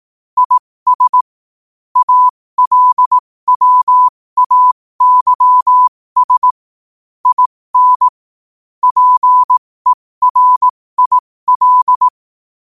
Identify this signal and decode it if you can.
Morse code, heard as a keyed tone, about 18 wpm, spaced as for 15 wpm, 1000 Hz. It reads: IS ALWAYS IN PERIL
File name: qotdfiles / 15wpm